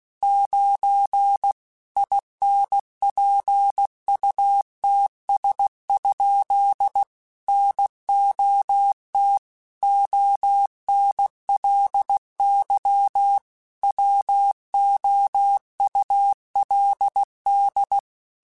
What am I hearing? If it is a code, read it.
Morse code, heard as a keyed tone, about 16 wpm, 800 Hz. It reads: 9 INPUTS? NOT ONLY WOULD
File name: Technical Subjects / Technical Subjects-mp3s / AND Gate @15